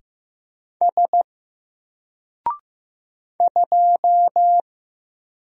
Morse code, heard as a keyed tone, about 15 wpm, 700 Hz. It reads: S 2
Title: Single Letter-Number Alt 15wpm